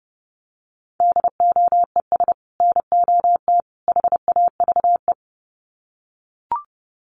Morse code, heard as a keyed tone, about 30 wpm, 700 Hz. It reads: DOES NOT HAVE
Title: Sets of 3 Words 30wpm